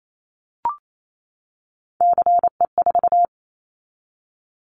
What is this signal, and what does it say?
Morse code, heard as a keyed tone, about 28 wpm, 700 Hz. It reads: CE4
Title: Callsign Prefixes for Top 100 DXCC - 28wpm